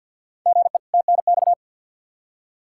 Morse code, heard as a keyed tone, about 53 wpm, 700 Hz. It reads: GE TNX
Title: Practice QSO - Exchange 1 - Beginning - 50wpm